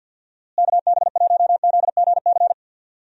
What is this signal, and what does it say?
Morse code, heard as a keyed tone, about 49 wpm, 700 Hz. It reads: KB1ZGC
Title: USA Call Signs - 6 Characters - Repeated 2x - 50wpm